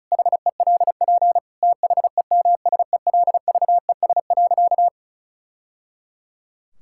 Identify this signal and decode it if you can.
Morse code, heard as a keyed tone, about 36 wpm, 700 Hz. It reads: HELP THEMSELVES.
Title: Sentences from Top 1000 Words - Review Part 5 - 35wpm